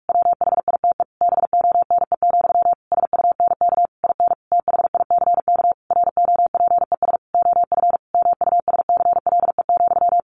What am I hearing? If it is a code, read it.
Morse code, heard as a keyed tone, about 46 wpm, 700 Hz. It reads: WHITE BONE, SUNK IN THICK ROPES OF MUSCLE,
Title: A Sound Of Thunder 45wpm